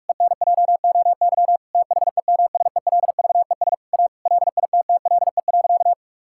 Morse code, heard as a keyed tone, about 46 wpm, 700 Hz. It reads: ENJOY THEMSELVES A LITTLE.
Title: Sentences from Top 1000 Words - Review Part 5 - 45wpm